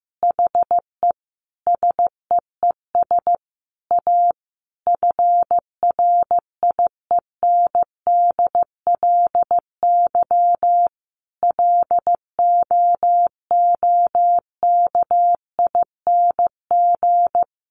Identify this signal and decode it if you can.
Morse code, heard as a keyed tone, about 15 wpm, 700 Hz. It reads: HE SEES A FRIENDLY LOOKING
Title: Easy Phrases 15wpm